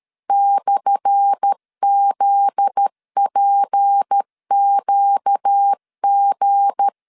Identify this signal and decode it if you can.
Morse code, heard as a keyed tone, about 13 wpm, 800 Hz. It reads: /ZPQG
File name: LEZIONI / 10 SIGNS / 10D COMA 12wpm 2544